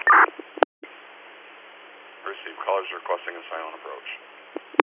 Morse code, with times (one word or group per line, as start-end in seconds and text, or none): none